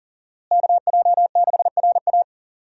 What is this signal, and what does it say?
Morse code, heard as a keyed tone, about 40 wpm, 690 Hz. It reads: KJ6RA
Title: USA Call Signs - 5 Characters - Repeated 3x - 40wpm